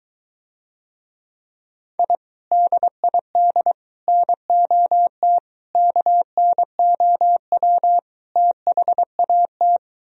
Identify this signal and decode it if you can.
Morse code, heard as a keyed tone, about 24 wpm, 700 Hz. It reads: I DID NOT KNOW THAT